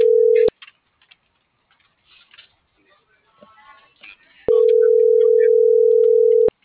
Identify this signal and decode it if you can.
Ringback tone (North America), 440 + 480 Hz.